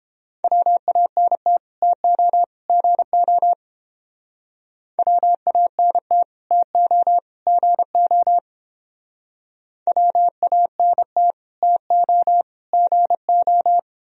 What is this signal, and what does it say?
Morse code, heard as a keyed tone, about 28 wpm, 700 Hz. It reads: WANT TO GO WANT TO GO WANT TO GO